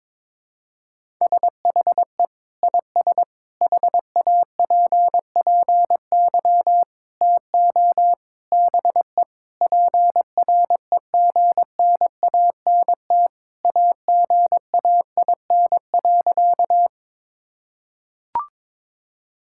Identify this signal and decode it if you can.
Morse code, heard as a keyed tone, about 22 wpm, 700 Hz. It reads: SHE IS HAPPY TO BE PREGNANT AGAIN.
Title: Sentences from Top 1000 Words - Review Part 6 - 22wpm